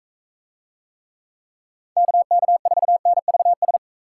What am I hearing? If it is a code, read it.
Morse code, heard as a keyed tone, about 39 wpm, 700 Hz. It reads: KK4NVS